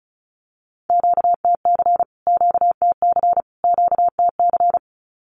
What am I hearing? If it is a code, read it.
Morse code, heard as a keyed tone, about 34 wpm, 700 Hz. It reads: QTC QTC QTC
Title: Q-Codes Repeated 3x 35wpm